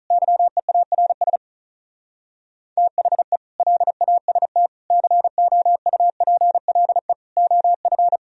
Morse code, heard as a keyed tone, about 46 wpm, 700 Hz. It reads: YEARS THE LAST COUPLE OF